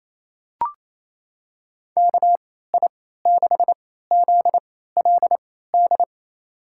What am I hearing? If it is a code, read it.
Morse code, heard as a keyed tone, about 28 wpm, spaced as for 16 wpm, 700 Hz. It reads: KI6ZLD
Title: USA Call Signs - 3x Character Spacing - 28wpm